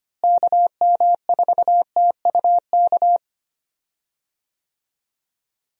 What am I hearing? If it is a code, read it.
Morse code, heard as a keyed tone, about 25 wpm, 700 Hz. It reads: KM4TUK
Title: USA Call Signs - 6 Characters - Repeated 3x - 25wpm